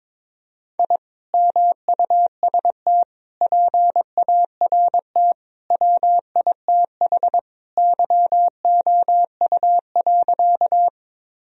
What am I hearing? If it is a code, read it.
Morse code, heard as a keyed tone, about 22 wpm, 700 Hz. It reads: I MUST PART WITH YOU.
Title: Sentences from Top 1000 Words - Review Part 1 - 22wpm